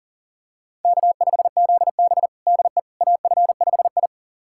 Morse code, heard as a keyed tone, about 40 wpm, 700 Hz. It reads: K5ZB DE AF5I
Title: USA Callsign DE Callsign - 4 Characters - 40wpm